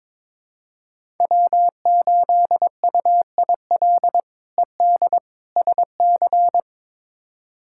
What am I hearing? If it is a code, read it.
Morse code, heard as a keyed tone, about 22 wpm, 700 Hz. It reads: W8UIL ED SC
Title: K1USN SST QSO Practice - 22wpm